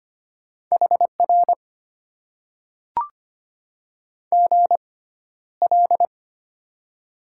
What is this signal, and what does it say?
Morse code, heard as a keyed tone, about 25 wpm, spaced as for 13 wpm, 700 Hz. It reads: HR GL
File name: Top 90 QSO Elements - 6x Character Spacing - 25wpm